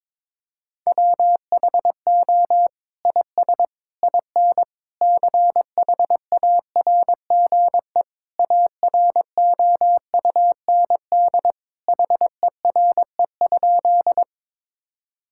Morse code, heard as a keyed tone, about 22 wpm, 700 Hz. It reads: WHO IS IN CHARGE AROUND HERE?
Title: Sentences from Top 1000 Words - Review Part 6 - 22wpm